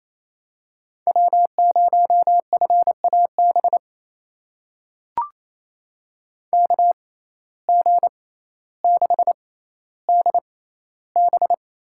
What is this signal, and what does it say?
Morse code, heard as a keyed tone, about 28 wpm, spaced as for 14 wpm, 700 Hz. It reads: W0FAB KG6DB
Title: USA Call Signs - 6x Character Spacing - 28wpm